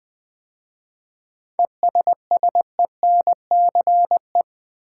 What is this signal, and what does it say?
Morse code, heard as a keyed tone, about 20 wpm, 700 Hz. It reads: ESSENCE
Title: 7 Letter Words - First 3 Letters Given - 20wpm